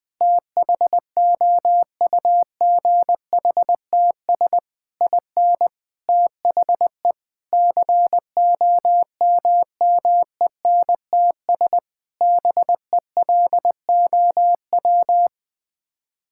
Morse code, heard as a keyed tone, about 21 wpm, 700 Hz. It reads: THOUGHTS IN THE COMMENTS BELOW
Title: Sets of 9 Words - 20wpm